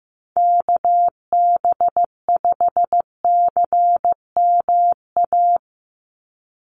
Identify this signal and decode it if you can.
Morse code, heard as a keyed tone, about 15 wpm, 700 Hz. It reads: KB5CMA
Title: USA Call Signs 15wpm